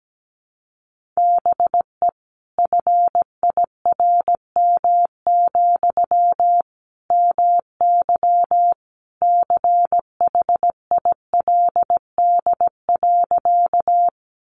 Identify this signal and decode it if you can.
Morse code, heard as a keyed tone, about 17 wpm, 700 Hz. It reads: BE FIRM, MY CHILD.
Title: Sentences from Top 1400 Words 17wpm